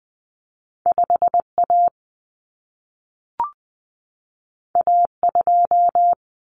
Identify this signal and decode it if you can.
Morse code, heard as a keyed tone, about 20 wpm, 700 Hz. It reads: 5A A2
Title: Common Cut Numbers - 2-Digit 20wpm